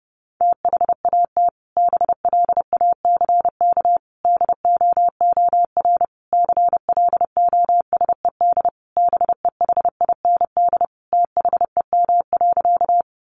Morse code, heard as a keyed tone, about 30 wpm, 700 Hz. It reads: THAT BLACK DOOR CLOSED BEHIND THEM.
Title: Sentences from Top 1400 Words 30wpm